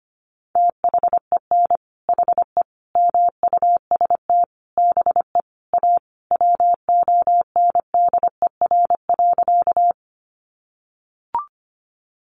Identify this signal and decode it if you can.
Morse code, heard as a keyed tone, about 25 wpm, 700 Hz. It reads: THEN HE MUST BE A WONDER.
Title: Sentences from Top 500 Words 25wpm